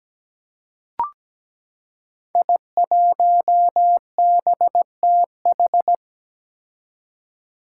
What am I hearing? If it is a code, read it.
Morse code, heard as a keyed tone, about 17 wpm, 700 Hz. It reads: I1BTH
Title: Five Challenging Letter-Number Combinations 17wpm